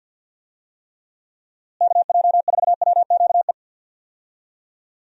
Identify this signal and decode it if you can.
Morse code, heard as a keyed tone, about 52 wpm, 700 Hz. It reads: KJ4WQE
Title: USA Call Signs - 6 Characters - Repeated 2x - 50wpm